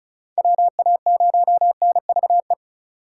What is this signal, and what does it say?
Morse code, heard as a keyed tone, about 36 wpm, 700 Hz. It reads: WA0NVE
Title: USA Call Signs - 3x Character Spacing - 35wpm